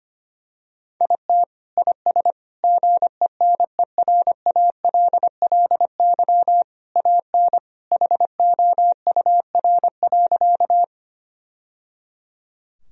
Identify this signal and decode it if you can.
Morse code, heard as a keyed tone, about 25 wpm, 700 Hz. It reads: IT IS GENERALLY AN HOUR.
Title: Sentences from Top 1000 Words - Review Part 4 - 25wpm